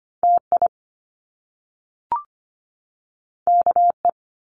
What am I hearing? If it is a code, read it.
Morse code, heard as a keyed tone, about 25 wpm, 700 Hz. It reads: TI KE
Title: Top Two Letter Combinations in Words 25wpm